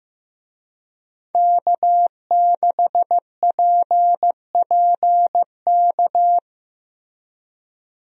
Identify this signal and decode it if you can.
Morse code, heard as a keyed tone, about 15 wpm, 700 Hz. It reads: K6PPK
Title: USA Call Signs - 5 Characters - Repeated 3x - 15wpm